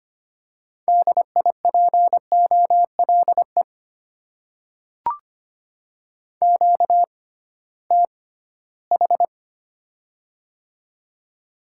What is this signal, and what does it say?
Morse code, heard as a keyed tone, about 25 wpm, spaced as for 16 wpm, 690 Hz. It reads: DIPOLE QTH
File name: Top 90 QSO Elements - 6x Character Spacing - 25wpm